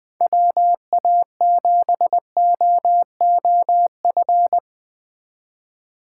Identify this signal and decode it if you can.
Morse code, heard as a keyed tone, about 22 wpm, 700 Hz. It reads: WA7OOF
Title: Speed Racing - USA Call Signs 20wpm